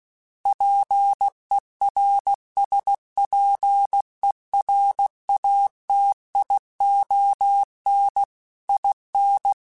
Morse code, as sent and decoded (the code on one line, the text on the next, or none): .--. .-. ... .--. .-. .- - .. --- -. / .. -.
PERSPERATION IN